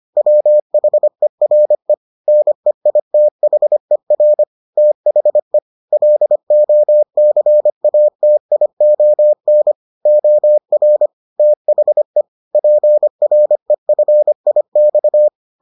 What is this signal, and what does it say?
Morse code, heard as a keyed tone, about 25 wpm, 600 Hz. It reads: WHERE NEITHER THE LOCATION OR THE PREFIX